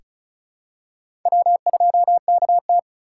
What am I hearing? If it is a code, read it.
Morse code, heard as a keyed tone, about 34 wpm, 700 Hz. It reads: W2KT
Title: USA Call Signs - 4 Characters - 4x Character Spacing - 35wpm